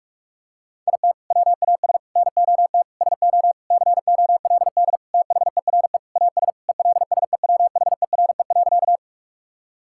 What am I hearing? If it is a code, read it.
Morse code, heard as a keyed tone, about 43 wpm, 700 Hz. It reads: IT WAS NOT SO COLD THERE AS ELSEWHERE.
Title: Sentences from Top 1600 Words 45wpm